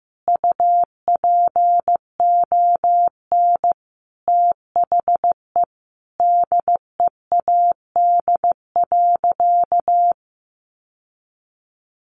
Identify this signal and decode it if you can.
Morse code, heard as a keyed tone, about 15 wpm, 700 Hz. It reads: UPON THE DEAD.
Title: Sentences from Top 1500 Words 15wpm